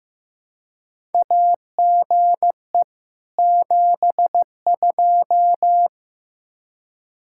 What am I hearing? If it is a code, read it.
Morse code, heard as a keyed tone, about 15 wpm, 700 Hz. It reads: AGE 72
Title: Practice QSO - Exchange 3 - More Info - 15wpm